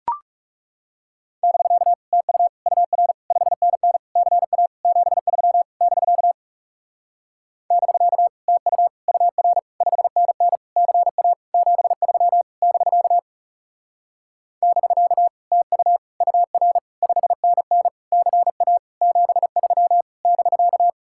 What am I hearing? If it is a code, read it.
Morse code, heard as a keyed tone, about 40 wpm, 700 Hz. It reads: <BK> TU UR 5NN CA 73 <BK> <BK> TU UR 5NN CA 73 <BK> <BK> TU UR 5NN CA 73 <BK>